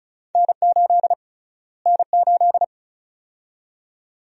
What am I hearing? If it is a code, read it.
Morse code, heard as a keyed tone, about 36 wpm, 700 Hz. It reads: N8 N8
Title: USA Call Sign Prefixes - Repeated 2x - 35wpm